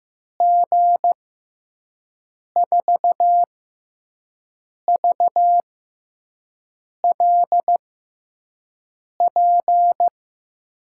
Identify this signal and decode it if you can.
Morse code, heard as a keyed tone, about 15 wpm, spaced as for 5 wpm, 700 Hz. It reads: G4VLP
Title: USA Call Signs - 6 Characters - 6x Character Spacing - 15wpm